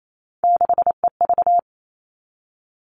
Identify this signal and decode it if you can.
Morse code, heard as a keyed tone, about 28 wpm, 700 Hz. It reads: 6EV
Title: All Cut Numbers - 3-Digit 28wpm